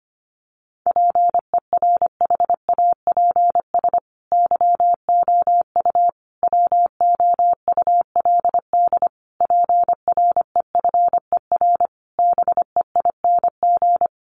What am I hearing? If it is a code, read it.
Morse code, heard as a keyed tone, about 25 wpm, 700 Hz. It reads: PERHAPS YOU WOULD PREFER BEING
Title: Sentences from Top 1100 Words 25wpm